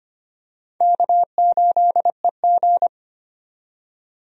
Morse code, heard as a keyed tone, about 25 wpm, 700 Hz. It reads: K8EG
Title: USA Call Signs - 4 Characters - 25wpm